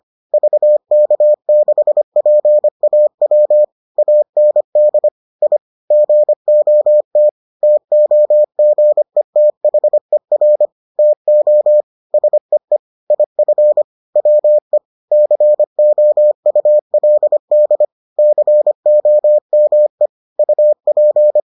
Morse code, heard as a keyed tone, about 25 wpm, 600 Hz. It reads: VK6PAW AND I GOT TOGETHER TO SEE IF WE COULD COME UP